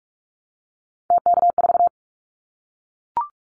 Morse code, heard as a keyed tone, about 44 wpm, 700 Hz. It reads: TK4
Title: Callsign Prefixes for Top 100 DXCC - 45wpm